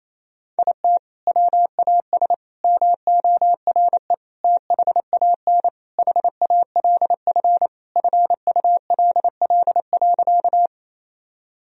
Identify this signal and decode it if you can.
Morse code, heard as a keyed tone, about 28 wpm, 700 Hz. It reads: IT WAS MORE THAN HALF FULL.
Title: Sentences from Top 1000 Words - Review Part 3 - 28wpm